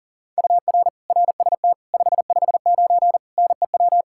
Morse code, heard as a keyed tone, about 37 wpm, 700 Hz. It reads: UR RST 559 NEW
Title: Practice QSO - Exchange 1 - Beginning - 40wpm